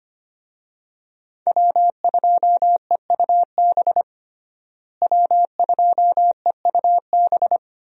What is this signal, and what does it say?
Morse code, heard as a keyed tone, about 25 wpm, 700 Hz. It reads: W2EUB W2EUB
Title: USA Call Signs - 5 Characters - Repeated 3x - 25wpm